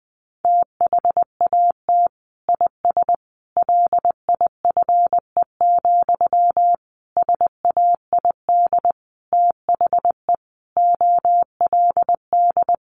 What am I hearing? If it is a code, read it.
Morse code, heard as a keyed tone, about 20 wpm, 700 Hz. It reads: THAT IS LIFE, SAID THE OLD
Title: Sentences from Top 900 Words 20wpm